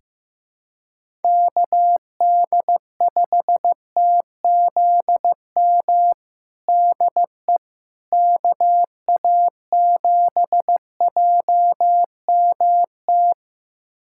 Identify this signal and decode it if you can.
Morse code, heard as a keyed tone, about 15 wpm, 700 Hz. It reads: KD5TZM DE KA7JMT
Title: USA Callsign DE Callsign 15wpm